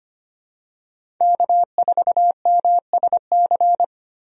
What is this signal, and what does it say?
Morse code, heard as a keyed tone, about 25 wpm, 690 Hz. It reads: K4MSC
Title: USA Call Signs - 5 Characters - 25wpm